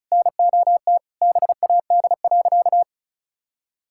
Morse code, heard as a keyed tone, about 34 wpm, 700 Hz. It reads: NOT BAD.
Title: Sentences from Top 1000 Words - Review Part 6 - 35wpm